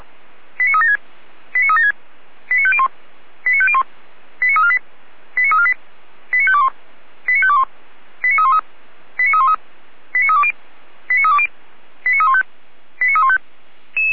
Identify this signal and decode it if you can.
Five-tone selective call (ZVEI), heard as quick runs of five tones, tones of about 70 ms. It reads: 89376 89376 89501 89501 89348 89348 89521 89521 89313 89313 89320 89320 89315 89315